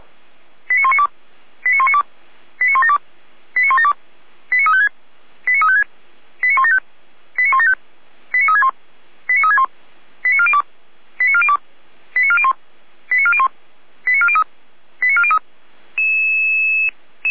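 Five-tone selective call (ZVEI), heard as quick runs of five tones, tones of about 80 ms. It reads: 89192 89192 89172 89172 89356 89356 89175 89175 89361 89361 89402 89402 89401 89401 89403 89403
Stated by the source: ZVEI five-tone call received by SDR.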